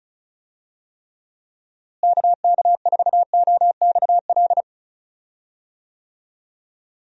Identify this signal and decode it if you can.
Morse code, heard as a keyed tone, about 36 wpm, 700 Hz. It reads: KK4OXL
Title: USA Call Signs - 6 Characters - Repeated 2x - 35wpm